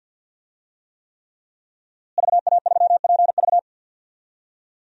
Morse code, heard as a keyed tone, about 46 wpm, 700 Hz. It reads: VA3PV
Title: DXCC-100 Call Signs - 5 Characters - Repeated 3x - 50wpm